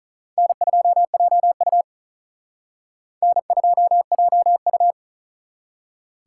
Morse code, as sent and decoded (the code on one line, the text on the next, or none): -. ..--- .--- ..- / -. ..--- .--- ..-
N2JU N2JU